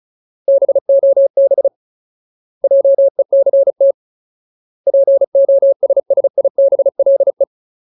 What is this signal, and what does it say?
Morse code, heard as a keyed tone, about 34 wpm, 550 Hz. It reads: BOB JECT POSSIBLE
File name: Ultimate Word List / 35WPM Ultimate